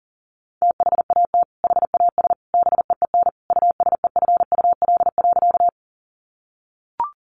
Morse code, heard as a keyed tone, about 37 wpm, 700 Hz. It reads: THAT HAS BEEN USEFUL.
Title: Sentences from Top 700 Words 40wpm